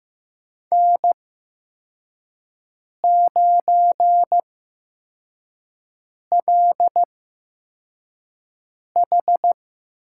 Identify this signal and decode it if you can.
Morse code, heard as a keyed tone, about 15 wpm, 700 Hz. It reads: N 9 L H
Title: USA Call Signs - 5 Characters - 8x Character Spacing - 15wpm